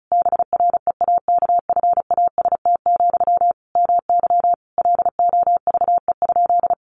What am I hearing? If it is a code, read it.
Morse code, heard as a keyed tone, about 34 wpm, 700 Hz. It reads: BREAKFAST, MY LOVE?
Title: Sentences from Top 1500 Words 35wpm